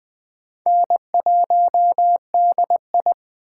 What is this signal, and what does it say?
Morse code, heard as a keyed tone, about 20 wpm, 700 Hz. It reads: N1DI
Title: USA Call Signs - 6x Character Spacing - 20wpm